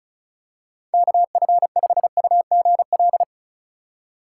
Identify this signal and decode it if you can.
Morse code, heard as a keyed tone, about 36 wpm, 700 Hz. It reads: KF5UGL
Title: USA Call Signs - 6 Characters - 3x Character Spacing - 35wpm